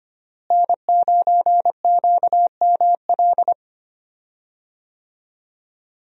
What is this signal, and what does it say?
Morse code, heard as a keyed tone, about 25 wpm, 700 Hz. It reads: N9QML
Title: USA Call Signs - 5 Characters - Repeated 2x - 25wpm